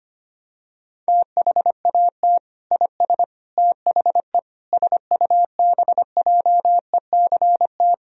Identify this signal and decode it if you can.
Morse code, heard as a keyed tone, about 25 wpm, 700 Hz. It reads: THAT IS THE SUBJECT
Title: Sentences from Top 1000 Words - Review Part 4 - 25wpm